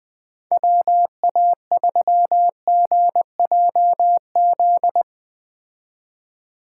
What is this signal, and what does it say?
Morse code, heard as a keyed tone, about 20 wpm, 700 Hz. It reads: WA3GJZ